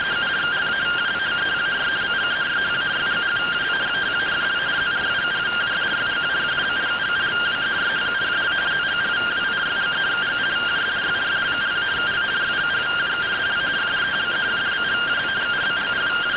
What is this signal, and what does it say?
Two-tone teleprinter signal, about 170 Hz shift.